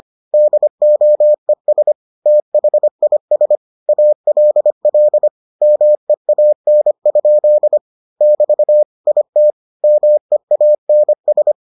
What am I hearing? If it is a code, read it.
Morse code, heard as a keyed tone, about 25 wpm, 600 Hz. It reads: DOES THIS ALL MEAN? <BT> IT MEANS